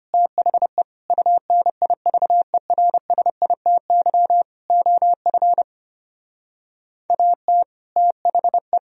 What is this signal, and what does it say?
Morse code, heard as a keyed tone, about 32 wpm, 700 Hz. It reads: THE UNIVERSITY OF AT THE